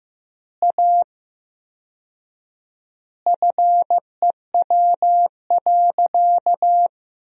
Morse code, heard as a keyed tone, about 15 wpm, 700 Hz. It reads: A FEW.
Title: 4x Word-Spacing - Sentences from Top 300 Words - 15wpm